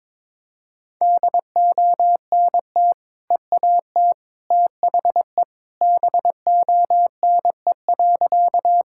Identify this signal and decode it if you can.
Morse code, heard as a keyed tone, about 22 wpm, 700 Hz. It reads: DONT EAT THE BONE.
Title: Sentences from Top 1600 Words 22wpm